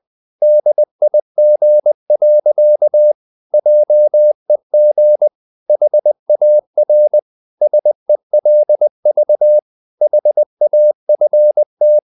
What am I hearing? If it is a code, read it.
Morse code, heard as a keyed tone, about 20 wpm, 600 Hz. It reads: DIG. JEG HAR SELV HAFT
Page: Ferie 20 wpm